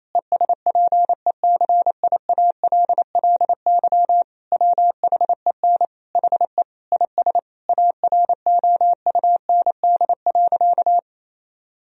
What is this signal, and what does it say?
Morse code, heard as a keyed tone, about 27 wpm, 700 Hz. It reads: ESPECIALLY WHEN HE IS AROUND.
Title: Sentences from Top 1000 Words - Review Part 4 - 28wpm